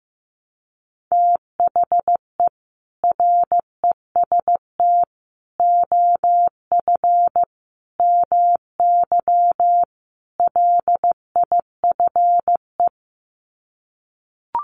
Morse code, heard as a keyed tone, about 15 wpm, 700 Hz. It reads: THE REST OF MY LIFE
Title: Sets of 5 Words 15wpm